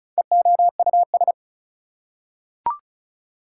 Morse code, heard as a keyed tone, about 36 wpm, 690 Hz. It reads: EOUS
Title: Prefixes and Suffixes 35wpm